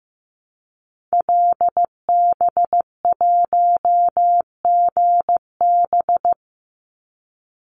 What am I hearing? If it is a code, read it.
Morse code, heard as a keyed tone, about 15 wpm, 700 Hz. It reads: LB1GB
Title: Call Signs Repeated 3x - 15wpm